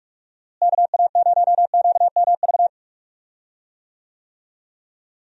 Morse code, heard as a keyed tone, about 46 wpm, 700 Hz. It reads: KA0QMV
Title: USA Call Signs - 6 Characters - Repeated 3x - 45wpm